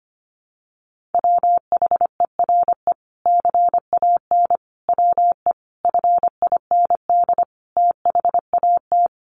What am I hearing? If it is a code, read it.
Morse code, heard as a keyed tone, about 25 wpm, 700 Hz. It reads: WHERE CAN WE FIND THAT
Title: Sentences from Top 1500 Words 25wpm